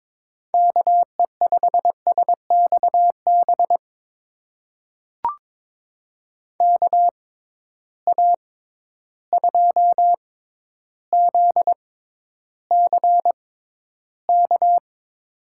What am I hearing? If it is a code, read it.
Morse code, heard as a keyed tone, about 21 wpm, spaced as for 11 wpm, 700 Hz. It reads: KE5SXB KA2ZCK
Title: USA Call Signs - 6 Characters - 6x Character Spacing - 22wpm